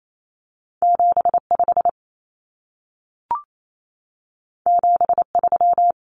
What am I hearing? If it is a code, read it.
Morse code, heard as a keyed tone, about 28 wpm, 700 Hz. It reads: 75 73
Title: Numbers - 2-Digit 28wpm